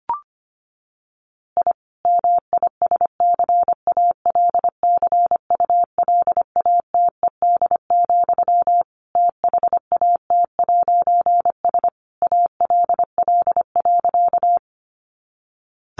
Morse code, heard as a keyed tone, about 25 wpm, 700 Hz. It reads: I MISCALCULATED, THAT'S ALL.
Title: A Sound Of Thunder 25wpm